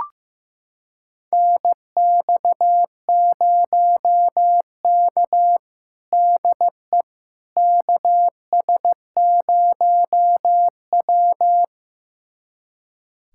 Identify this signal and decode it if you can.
Morse code, heard as a keyed tone, about 15 wpm, 700 Hz. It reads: NX0K DE KS0W
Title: USA Callsign DE Callsign - 4 Characters - 15wpm